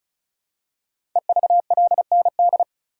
Morse code, heard as a keyed tone, about 36 wpm, 700 Hz. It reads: EVLND
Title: Code Groups - 5 Letter:Numbers - 35wpm